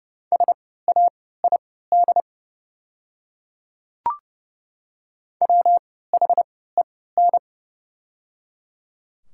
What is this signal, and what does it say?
Morse code, heard as a keyed tone, about 30 wpm, spaced as for 17 wpm, 700 Hz. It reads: SAID WHEN
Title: 3x Character Spacing - Top 100 Words - 30wpm